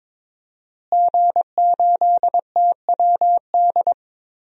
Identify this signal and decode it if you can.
Morse code, heard as a keyed tone, about 22 wpm, 700 Hz. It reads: G8TWD
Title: DXCC-100 Call Signs - 5 Characters - Repeated 2x - 22wpm